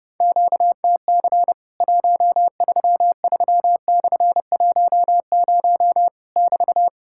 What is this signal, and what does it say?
Morse code, heard as a keyed tone, about 31 wpm, 700 Hz. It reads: QTC 133/10 <BT>
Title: Worked All Europe DX Contest - QTC Exchange - 30wpm